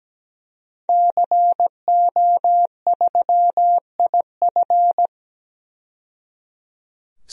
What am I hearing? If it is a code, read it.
Morse code, heard as a keyed tone, about 17 wpm, 700 Hz. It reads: CO3IF
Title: DXCC-100 Call Signs - 5 Characters - 17wpm